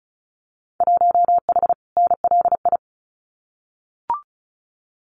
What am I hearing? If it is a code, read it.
Morse code, heard as a keyed tone, about 36 wpm, 690 Hz. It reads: 1H NLI
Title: Winter Field Day Exchanges 35wpm